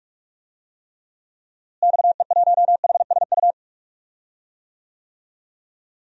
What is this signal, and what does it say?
Morse code, heard as a keyed tone, about 44 wpm, 700 Hz. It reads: XE1HSU
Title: DXCC-100 Call Signs - 6 Characters - Repeated 3x - 45wpm